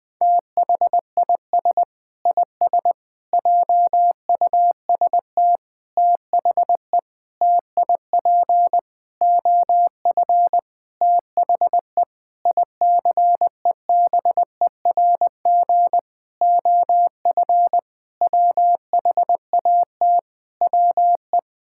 Morse code, heard as a keyed tone, about 20 wpm, 700 Hz. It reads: THIS IS JUST THE TIP OF THE ICEBERG OF WHAT WE
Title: Sentences with 8-Word Phrases - Spoken Beginning - 20wpm